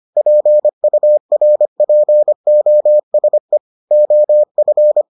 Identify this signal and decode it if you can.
Morse code, heard as a keyed tone, about 25 wpm, 600 Hz. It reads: PURPOSE OF